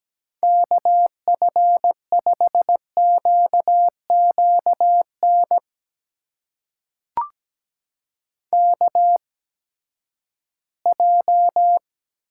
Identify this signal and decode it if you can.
Morse code, heard as a keyed tone, about 17 wpm, spaced as for 12 wpm, 700 Hz. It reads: KF5QQN KJ
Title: USA Call Signs - 6 Characters - 8x Character Spacing - 17wpm